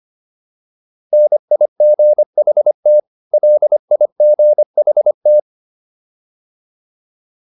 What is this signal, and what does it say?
Morse code, heard as a keyed tone, about 25 wpm, 600 Hz. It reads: NIGHT LIGHT